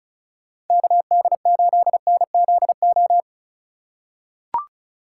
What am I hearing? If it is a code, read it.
Morse code, heard as a keyed tone, about 36 wpm, 700 Hz. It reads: KD8NZO
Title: USA Call Signs - 6 Characters - 3x Character Spacing - 35wpm